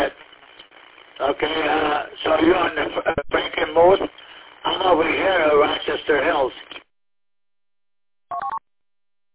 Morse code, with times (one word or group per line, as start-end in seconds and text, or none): none